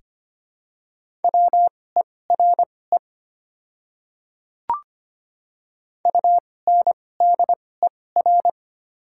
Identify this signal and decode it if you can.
Morse code, heard as a keyed tone, about 25 wpm, spaced as for 18 wpm, 700 Hz. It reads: WERE UNDER
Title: 2x Character Spacing - Top 200 Words - 25wpm